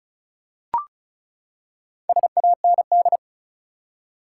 Morse code, heard as a keyed tone, about 34 wpm, 700 Hz. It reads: SAND
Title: Four Letter Words - 35wpm